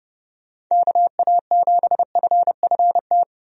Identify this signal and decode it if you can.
Morse code, heard as a keyed tone, about 30 wpm, 700 Hz. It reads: KA7FFT
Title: USA Call Signs - 6 Characters - 30wpm